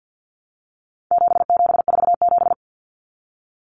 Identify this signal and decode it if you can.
Morse code, heard as a keyed tone, about 50 wpm, 700 Hz. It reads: 7747
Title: Common Cut Numbers - 4-Digit 50wpm